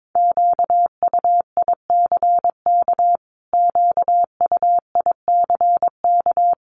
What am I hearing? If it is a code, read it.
Morse code, heard as a keyed tone, about 22 wpm, 700 Hz. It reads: QUICK QUICK
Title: Top 500 Words Repeated 22wpm